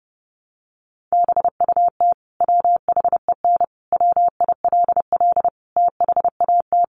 Morse code, heard as a keyed tone, about 28 wpm, 700 Hz. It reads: BUT WHEN WILL THAT
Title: Sentences from Top 100 Words 30wpm